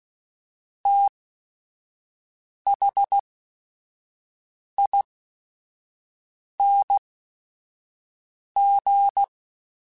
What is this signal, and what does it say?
Morse code, heard as a keyed tone, about 16 wpm, spaced as for 5 wpm, 800 Hz. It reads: THING
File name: Special INGs / Special INGs-mp3s / Sngl Ing @5